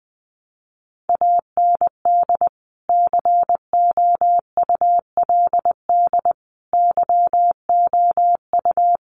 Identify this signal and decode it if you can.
Morse code, heard as a keyed tone, about 20 wpm, 700 Hz. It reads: AND COULD YOU
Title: Sentences from Top 1200 Words 20wpm